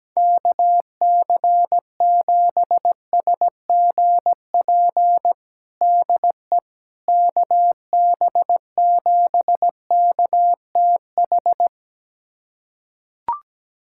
Morse code, heard as a keyed tone, about 17 wpm, 700 Hz. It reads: KC7SGP DE KB7KTH
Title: USA Callsign DE Callsign 17wpm